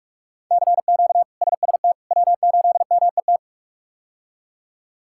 Morse code, heard as a keyed tone, about 46 wpm, 700 Hz. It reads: CQ SST W8MET
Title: K1USN SST QSO Practice - 45wpm